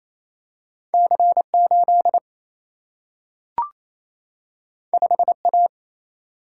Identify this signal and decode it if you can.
Morse code, heard as a keyed tone, about 28 wpm, 700 Hz. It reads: C8 5A
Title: Single-Letter Single-Number 28wpm